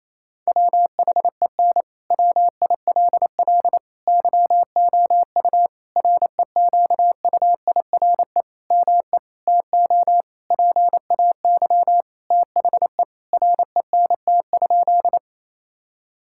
Morse code, heard as a keyed tone, about 27 wpm, 700 Hz. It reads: WHEN WILL YOU REQUIRE ME TO PAY THE RENT?
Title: Sentences from Top 1000 Words - Review Part 6 - 28wpm